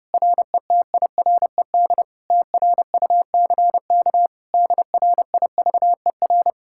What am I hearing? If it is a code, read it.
Morse code, heard as a keyed tone, about 33 wpm, 700 Hz. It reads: RETIRED TRUCK DRIVER